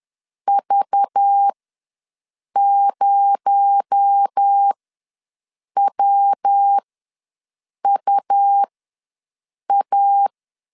Morse code, heard as a keyed tone, about 10 wpm, spaced as for 6 wpm, 800 Hz. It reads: V0WUA
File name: LEZIONI / 7 GF9Z / 7 G ws 10wpm 3843